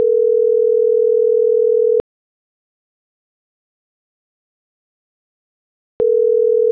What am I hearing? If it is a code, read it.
Ringback tone (North America), 440 + 480 Hz.